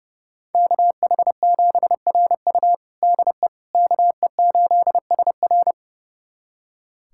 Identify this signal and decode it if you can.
Morse code, heard as a keyed tone, about 32 wpm, 700 Hz. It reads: KH7RU DE KE8SR
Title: USA Callsign DE Callsign - 5 Characters - 30wpm